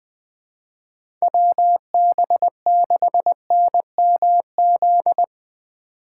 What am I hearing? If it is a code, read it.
Morse code, heard as a keyed tone, about 20 wpm, 700 Hz. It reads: WB6NMZ
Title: USA Call Signs - 6 Characters - 3x Character Spacing - 20wpm